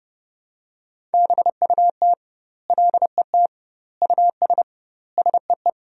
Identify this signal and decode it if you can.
Morse code, heard as a keyed tone, about 30 wpm, 700 Hz. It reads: BUT LET US SEE
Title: 2x Word-Spacing - Sentences from Top 500 Words - 30wpm